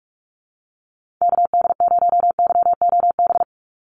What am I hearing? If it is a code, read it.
Morse code, heard as a keyed tone, about 44 wpm, 700 Hz. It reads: KD0YOB
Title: NA SOTA Activator Callsigns 45wpm